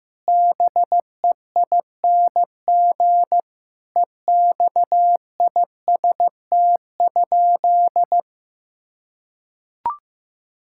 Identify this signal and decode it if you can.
Morse code, heard as a keyed tone, about 15 wpm, 700 Hz. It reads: BEING EXIST?
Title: Sentences from Top 1000 Words - Review Part 5 - 15wpm